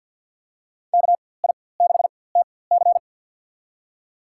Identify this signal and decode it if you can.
Morse code, heard as a keyed tone, about 48 wpm, spaced as for 23 wpm, 700 Hz. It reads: KI6TC
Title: USA Call Signs - 5 Characters - 4x Character Spacing - 50wpm